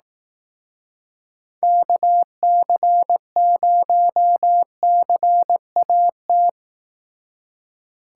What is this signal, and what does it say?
Morse code, heard as a keyed tone, about 18 wpm, 700 Hz. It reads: KC0CAT
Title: Speed Racing - USA 2x3 Call Signs 15wpm